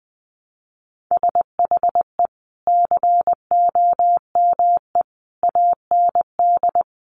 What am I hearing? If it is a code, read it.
Morse code, heard as a keyed tone, about 19 wpm, 700 Hz. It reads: SHE COME AND
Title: Sentences from Top 1100 Words 20wpm